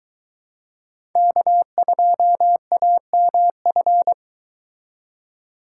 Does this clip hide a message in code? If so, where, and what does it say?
Yes: Morse code from 1.2 to 4.1 s, heard as a keyed tone, about 23 wpm, 700 Hz. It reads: K2AMF